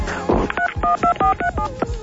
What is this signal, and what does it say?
DTMF, heard as telephone touch tones, tones of about 100 ms, with speech. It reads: 312437